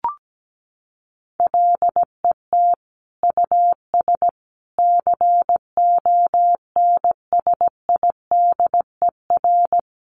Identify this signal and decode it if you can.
Morse code, heard as a keyed tone, about 17 wpm, 700 Hz. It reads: LET US CONSIDER
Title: Sentences from Top 1300 Words 17wpm